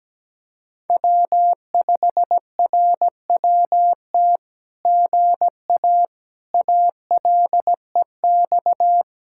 Morse code, heard as a keyed tone, about 17 wpm, 700 Hz. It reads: W5RWT GA ALEX
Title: SOTA QSO Practice 17wpm